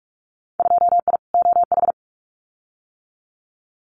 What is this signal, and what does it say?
Morse code, heard as a keyed tone, about 46 wpm, 700 Hz. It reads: 2I OH
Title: Winter Field Day Exchanges 45wpm